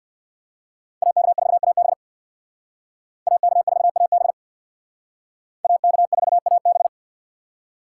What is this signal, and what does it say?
Morse code, heard as a keyed tone, about 60 wpm, 700 Hz. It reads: AK4AB AK4AB AK4AB
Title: Speed Racing - USA Call Signs 50wpm